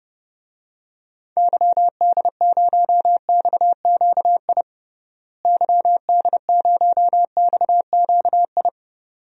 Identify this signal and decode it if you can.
Morse code, heard as a keyed tone, about 30 wpm, 700 Hz. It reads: YD0XQI YD0XQI
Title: DXCC-100 Call Signs - 6 Characters - Repeated 2x - 30wpm